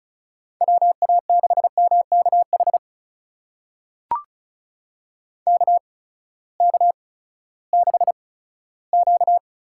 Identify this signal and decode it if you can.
Morse code, heard as a keyed tone, about 36 wpm, spaced as for 18 wpm, 700 Hz. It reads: WA6MKH KK6Q
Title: USA Call Signs - 6 Characters - 8x Character Spacing - 35wpm